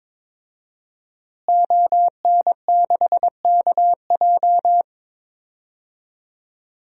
Morse code, heard as a keyed tone, about 22 wpm, 700 Hz. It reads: ON6KJ
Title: DXCC-100 Call Signs - 5 Characters - Repeated 2x - 22wpm